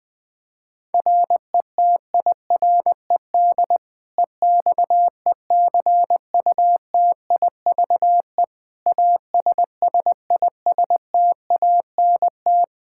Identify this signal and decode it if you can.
Morse code, heard as a keyed tone, about 20 wpm, 700 Hz. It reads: RETIRED EXECUTIVE ASSISTANT
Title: Practice QSO - Exchange 3 - More Info - 20wpm